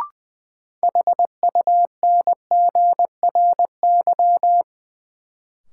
Morse code, heard as a keyed tone, about 20 wpm, 700 Hz. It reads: HUNGRY
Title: Sentences from Top 1400 Words - New Word Drill - 20wpm